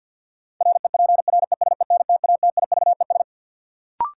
Morse code, heard as a keyed tone, about 50 wpm, 700 Hz. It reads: REPRESENTATIVES
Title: Sets of 5 Words Encore - New Word Drill - 50wpm